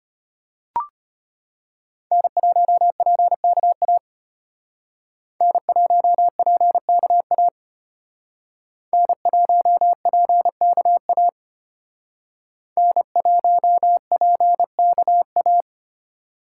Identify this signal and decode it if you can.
Morse code, heard as a keyed tone, about 32 wpm, 700 Hz. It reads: N1PKA N1PKA N1PKA N1PKA